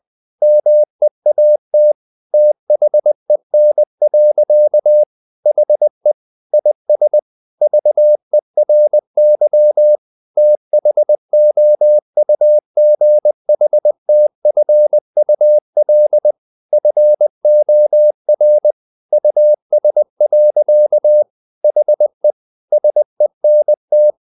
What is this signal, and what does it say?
Morse code, heard as a keyed tone, about 20 wpm, 600 Hz. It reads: MEAT THEN. HE IS VERY THOUGHTFUL FOR US. HE SENT
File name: the old man and the sea / the old man and the sea part 1 20wpm